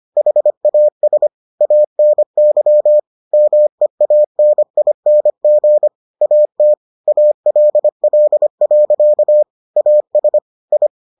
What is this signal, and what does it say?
Morse code, heard as a keyed tone, about 25 wpm, 600 Hz. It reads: HAS ANY MEANING AT ALL. AS I